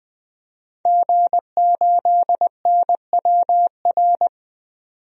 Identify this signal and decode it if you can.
Morse code, heard as a keyed tone, about 21 wpm, 700 Hz. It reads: G8NWR
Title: DXCC-100 Call Signs - 5 Characters - 20wpm